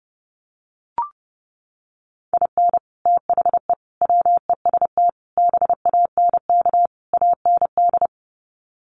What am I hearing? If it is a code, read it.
Morse code, heard as a keyed tone, about 30 wpm, 700 Hz. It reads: IN THE WEST BANK AND
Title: Sets of 5 Words Encore2 30wpm